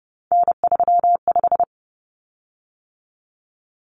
Morse code, heard as a keyed tone, about 30 wpm, 700 Hz. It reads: N35
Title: Common Cut Numbers - 3-Digit 30wpm